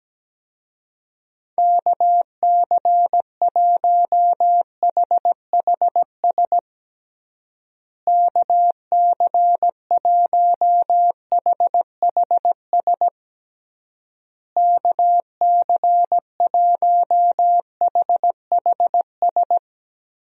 Morse code, heard as a keyed tone, about 17 wpm, 700 Hz. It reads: KC1HHS KC1HHS KC1HHS
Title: USA Call Signs - 6 Characters - Repeated 3x - 17wpm